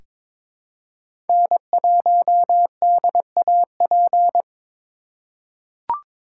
Morse code, heard as a keyed tone, about 22 wpm, 700 Hz. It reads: N1DAP
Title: USA Call Signs - 5 Characters - 4x Character Spacing - 22wpm